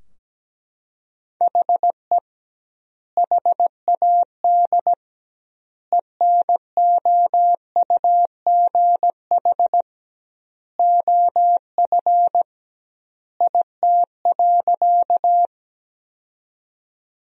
Morse code, heard as a keyed tone, about 17 wpm, 700 Hz. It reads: HE HAD ENOUGH OF IT.
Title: 2x Word-Spacing - Sentences from Top 400 Words - 17wpm